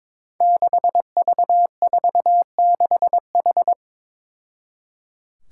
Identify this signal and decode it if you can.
Morse code, heard as a keyed tone, about 22 wpm, 700 Hz. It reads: 6V46H
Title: Five Challenging Letter-Number Combinations 22wpm